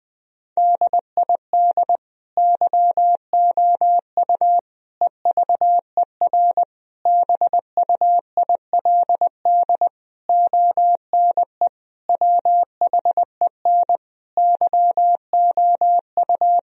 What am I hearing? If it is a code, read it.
Morse code, heard as a keyed tone, about 21 wpm, 700 Hz. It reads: DID YOU EVER BUILD ONE WHEN YOU
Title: Sentences from Top 1000 Words - Review Part 1 - 20wpm